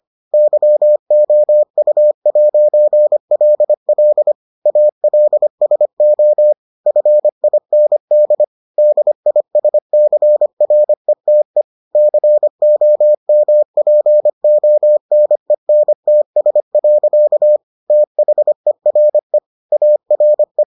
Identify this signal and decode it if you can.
Morse code, heard as a keyed tone, about 25 wpm, 600 Hz. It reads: YOU'LL ALSO FIND DISCRETE COMPONENTS. THERE ARE